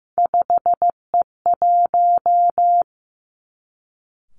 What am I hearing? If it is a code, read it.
Morse code, heard as a keyed tone, about 15 wpm, 700 Hz. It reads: 5E1
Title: Three Challenging Letter-Number Combinations 15wpm